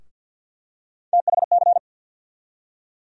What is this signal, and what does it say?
Morse code, heard as a keyed tone, about 51 wpm, 700 Hz. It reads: THC
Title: USA Call Sign Postfixes - 3 Letters 50wpm